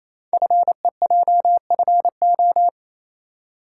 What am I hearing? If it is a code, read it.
Morse code, heard as a keyed tone, about 27 wpm, 700 Hz. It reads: FEJFO
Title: Code Groups - 5 Letters - 28wpm